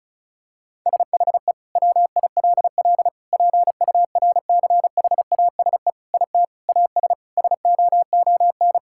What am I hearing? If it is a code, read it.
Morse code, heard as a keyed tone, about 34 wpm, 700 Hz. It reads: SHE WILL PURCHASE IT AS SOON
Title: Sentences from Top 1000 Words - Review Part 5 - 35wpm